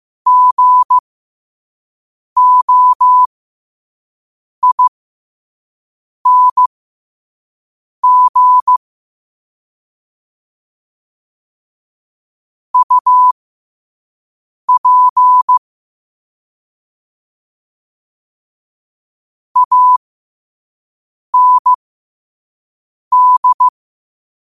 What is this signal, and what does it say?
Morse code, heard as a keyed tone, about 15 wpm, spaced as for 5 wpm, 1000 Hz. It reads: GOING UP AND
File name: qotdfiles / 5wpm